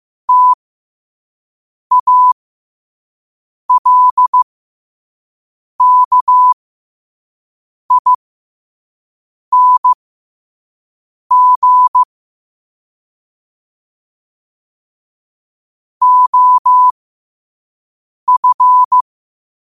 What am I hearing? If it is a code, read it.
Morse code, heard as a keyed tone, about 15 wpm, spaced as for 5 wpm, 1000 Hz. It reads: TALKING OF
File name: qotdfiles / 5wpm